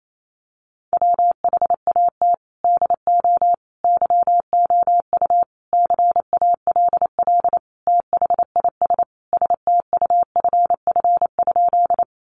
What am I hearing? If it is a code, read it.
Morse code, heard as a keyed tone, about 28 wpm, 700 Hz. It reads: WHAT DO YOU CALL THIS STUFF?
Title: Sentences from Top 1400 Words 28wpm